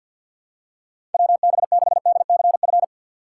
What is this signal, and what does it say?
Morse code, heard as a keyed tone, about 48 wpm, 700 Hz. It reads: WB6DCF
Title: USA Call Signs - 2x Character Spacing - 50wpm